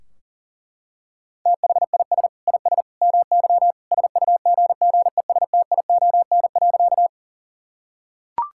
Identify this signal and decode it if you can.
Morse code, heard as a keyed tone, about 40 wpm, 700 Hz. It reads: THIS IS MY SUGGESTION.
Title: Sentences from Top 1600 Words 40wpm